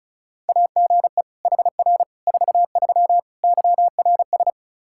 Morse code, heard as a keyed tone, about 36 wpm, 700 Hz. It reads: AGE HR 43 YRS
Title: K1USN SST QSO Practice 35wpm